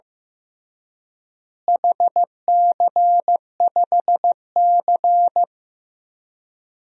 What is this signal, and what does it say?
Morse code, heard as a keyed tone, about 16 wpm, 700 Hz. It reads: HC5C
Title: DXCC-100 Call Signs - 4 Characters - Repeated 3x - 15wpm